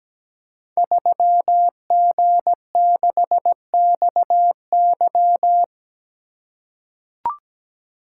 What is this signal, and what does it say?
Morse code, heard as a keyed tone, about 17 wpm, 700 Hz. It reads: 3G6XY
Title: Code Groups - 5 Letter:Numbers - 17wpm